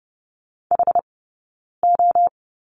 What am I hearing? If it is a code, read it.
Morse code, heard as a keyed tone, about 30 wpm, spaced as for 9 wpm, 690 Hz. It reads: HO
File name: course / 30:10wpm / 086 Lesson 15d - Number 5 - Two Characters 30wpm